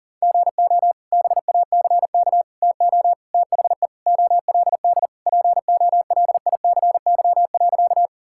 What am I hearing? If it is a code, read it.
Morse code, heard as a keyed tone, about 40 wpm, 700 Hz. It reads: GO BACK TO THE OLD POLICY.
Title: Sentences from Top 1000 Words - Review Part 5 - 40wpm